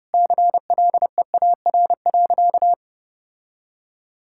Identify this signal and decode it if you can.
Morse code, heard as a keyed tone, about 32 wpm, 700 Hz. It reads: CLEAR.
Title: Sentences from Top 1600 Words 30wpm